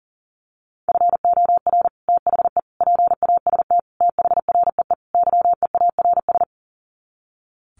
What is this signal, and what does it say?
Morse code, heard as a keyed tone, about 40 wpm, 700 Hz. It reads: FOR THE PAST THREE YEARS
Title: Sets of 5 Words Encore2 40wpm